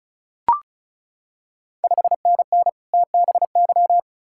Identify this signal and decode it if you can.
Morse code, heard as a keyed tone, about 36 wpm, 700 Hz. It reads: 5NN TBY
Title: Ontario QSO Party Exchange 35wpm